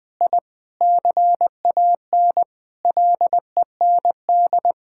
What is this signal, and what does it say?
Morse code, heard as a keyed tone, about 21 wpm, 700 Hz. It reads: I CAN LEND
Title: Sentences with 5-Word Phrases - 20wpm